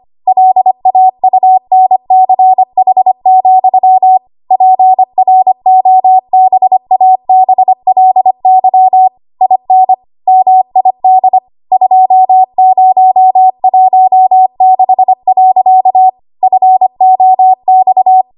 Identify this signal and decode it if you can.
Morse code, heard as a keyed tone, about 25 wpm, 750 Hz. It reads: LAUNCH, PROBABLY IN MID 2016. FOX
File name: Archive / 25 WPM / 25WPM